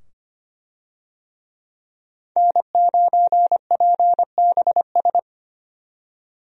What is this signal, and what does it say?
Morse code, heard as a keyed tone, about 25 wpm, 700 Hz. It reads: N9PBS
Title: USA Call Signs - 5 Characters - 25wpm